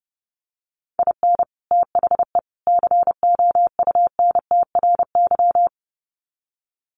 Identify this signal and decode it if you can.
Morse code, heard as a keyed tone, about 30 wpm, 700 Hz. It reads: IN THE COUNTRY
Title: Sets of 3 Words 30wpm